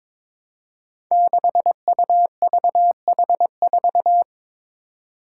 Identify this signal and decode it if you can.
Morse code, heard as a keyed tone, about 21 wpm, 700 Hz. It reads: 6UVH4
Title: Five Challenging Letter-Number Combinations 22wpm